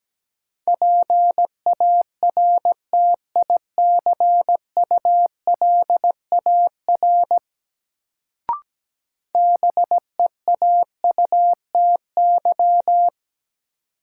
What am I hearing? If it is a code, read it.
Morse code, heard as a keyed tone, about 17 wpm, 700 Hz. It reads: PARTICULAR BEAUTY
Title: Sentences from Top 600 Words - New Word Drill - 17wpm